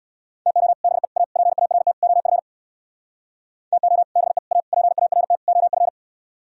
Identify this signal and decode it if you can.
Morse code, heard as a keyed tone, about 64 wpm, 700 Hz. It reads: TO BE A PART OF TO BE A PART OF